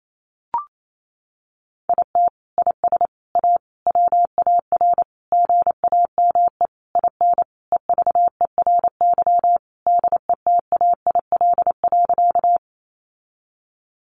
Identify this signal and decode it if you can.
Morse code, heard as a keyed tone, about 28 wpm, 700 Hz. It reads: IT IS A WAR GAME IN EVERY DETAIL.
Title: Sentences from Top 1500 Words 28wpm